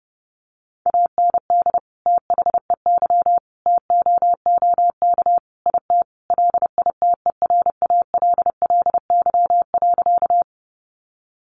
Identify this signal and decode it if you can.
Morse code, heard as a keyed tone, about 30 wpm, 700 Hz. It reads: AND THEY TOOK IT LITERALLY.
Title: Sentences from Top 1400 Words 30wpm